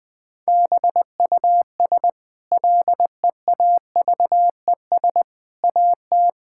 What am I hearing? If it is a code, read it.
Morse code, heard as a keyed tone, about 20 wpm, 700 Hz. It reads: BUS LEAVES AT
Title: Easy Sentences with Hints - 20wpm